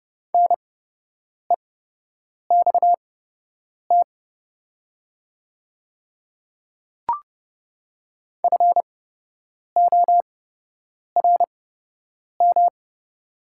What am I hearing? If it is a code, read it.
Morse code, heard as a keyed tone, about 30 wpm, spaced as for 8 wpm, 700 Hz. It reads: NEXT FORM
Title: 8x Character Spacing - Top 500 Words - 30wpm